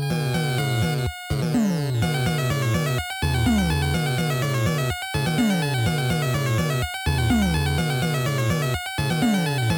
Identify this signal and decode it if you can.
There is no code.